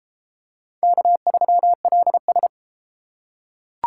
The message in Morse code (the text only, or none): K3LS